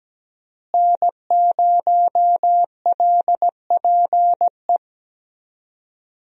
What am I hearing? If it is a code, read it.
Morse code, heard as a keyed tone, about 17 wpm, 700 Hz. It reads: N0LPE
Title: USA Call Signs - 5 Characters - Repeated 2x - 17wpm